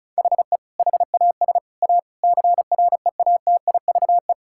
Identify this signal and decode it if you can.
Morse code, heard as a keyed tone, about 36 wpm, 700 Hz. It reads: HE HAS A CREATIVE
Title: Sentences from Top 1000 Words - Review Part 5 - 35wpm